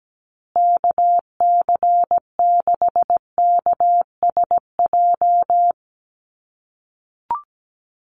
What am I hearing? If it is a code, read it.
Morse code, heard as a keyed tone, about 17 wpm, 700 Hz. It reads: KC6KSJ
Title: USA Call Signs Repeated 3x 17wpm